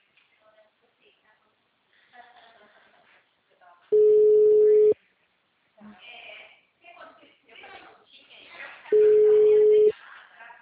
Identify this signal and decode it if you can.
Ringback tone (Europe, 425 Hz), with speech.